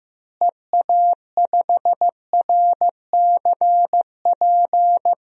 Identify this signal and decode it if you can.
Morse code, heard as a keyed tone, about 15 wpm, 700 Hz. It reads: EA5RCP
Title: DXCC-100 Call Signs - 6 Characters - Repeated 3x - 15wpm